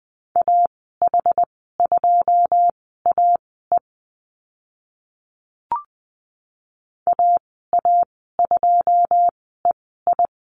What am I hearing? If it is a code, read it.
Morse code, heard as a keyed tone, about 20 wpm, spaced as for 15 wpm, 700 Hz. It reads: AH2AE AA2EI
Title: 082 Lesson 14g - Number 2 - Call Signs 20wpm